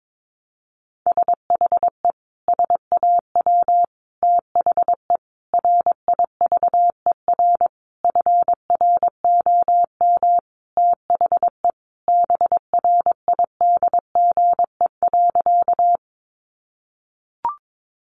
Morse code, heard as a keyed tone, about 22 wpm, 700 Hz. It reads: SHE SAW THE RIVER FROM THE BRIDGE.
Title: Sentences from Top 1200 Words 22wpm